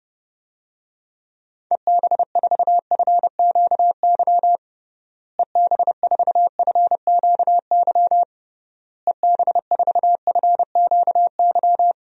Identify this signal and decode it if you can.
Morse code, heard as a keyed tone, about 29 wpm, 700 Hz. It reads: EB4FQY EB4FQY EB4FQY
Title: DXCC-100 Call Signs - 6 Characters - Repeated 3x - 30wpm